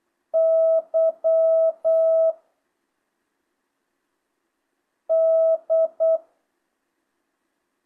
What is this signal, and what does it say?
Morse code, heard as a keyed tone, about 8 wpm, 1300 Hz. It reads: Y D
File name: jh8cbh / 2025cw